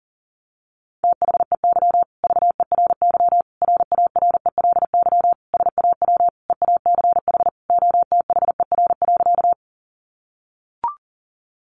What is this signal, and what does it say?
Morse code, heard as a keyed tone, about 40 wpm, 700 Hz. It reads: THEY VERY RARELY SAW EACH OTHER.
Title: Sentences from Top 1400 Words 40wpm